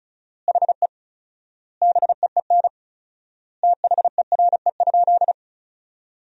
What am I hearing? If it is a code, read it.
Morse code, heard as a keyed tone, about 36 wpm, 700 Hz. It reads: HE BEEN THERE?
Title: 4x Word-Spacing - Sentences from Top 100 Words - 35wpm